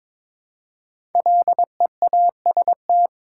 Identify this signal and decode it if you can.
Morse code, heard as a keyed tone, about 22 wpm, 700 Hz. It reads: LEAST
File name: 5 Letter Words - First 3 Letters Given - 22wpm